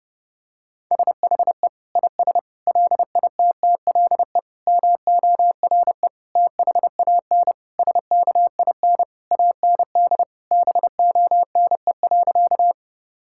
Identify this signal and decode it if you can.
Morse code, heard as a keyed tone, about 30 wpm, 700 Hz. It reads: SHE IS LITTLE MORE THAN SKIN AND BONE.
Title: Sentences from Top 1600 Words 30wpm